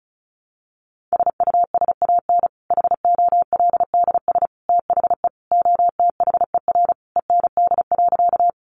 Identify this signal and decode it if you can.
Morse code, heard as a keyed tone, about 34 wpm, 700 Hz. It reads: SUSAN HOLDS THE OTHER END.
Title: Easy Phrases 35wpm